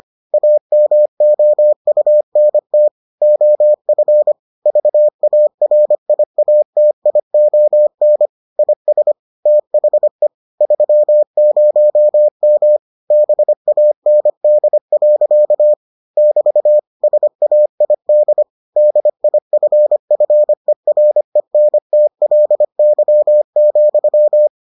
Morse code, heard as a keyed tone, about 25 wpm, 600 Hz. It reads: AMOUNT OF VARIATION IS THE 30M BAND. <BT> SAID DIFFERENTLY,